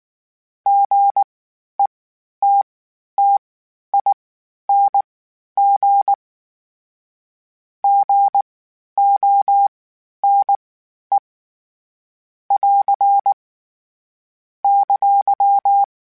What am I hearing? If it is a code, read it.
Morse code, heard as a keyed tone, about 19 wpm, spaced as for 11 wpm, 800 Hz. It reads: GETTING GONE <AR> !
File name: Short Runs / Short Runs-mp3s / Gotten @18